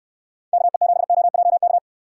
Morse code, heard as a keyed tone, about 60 wpm, 700 Hz. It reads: KE7OJK